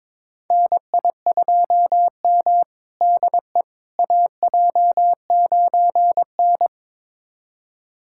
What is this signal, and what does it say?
Morse code, heard as a keyed tone, about 22 wpm, 700 Hz. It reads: NI2M DE AJ9N
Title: USA Callsign DE Callsign - 4 Characters - 22wpm